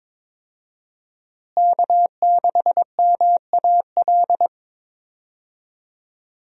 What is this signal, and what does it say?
Morse code, heard as a keyed tone, about 22 wpm, 700 Hz. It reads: K6MAL
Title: USA Call Signs - 5 Characters - Repeated 3x - 22wpm